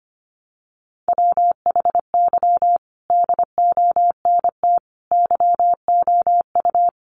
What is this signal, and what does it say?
Morse code, heard as a keyed tone, about 25 wpm, 700 Hz. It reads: WHY DONT YOU
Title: Sentences from Top 300 Words 25wpm